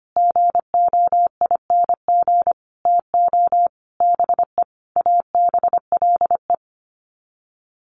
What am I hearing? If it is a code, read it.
Morse code, heard as a keyed tone, about 25 wpm, 700 Hz. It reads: GOING TO BE ABLE
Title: Sets of 5 Words 25wpm